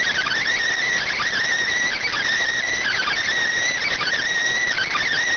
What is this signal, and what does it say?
Emergency Alert System (SAME) tones, heard as harsh buzzing bursts. It is heard from 0.8 to 4.9 s.